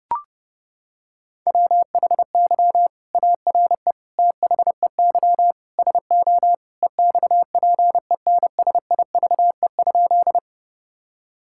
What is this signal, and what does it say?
Morse code, heard as a keyed tone, about 30 wpm, 700 Hz. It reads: WHY ARE THEY SO EXPENSIVE?
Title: Sentences from Top 1000 Words - Review Part 6 - 30wpm